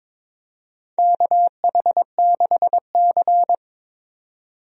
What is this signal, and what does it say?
Morse code, heard as a keyed tone, about 22 wpm, 700 Hz. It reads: KH6C
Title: DXCC-100 Call Signs - 4 Characters - 22wpm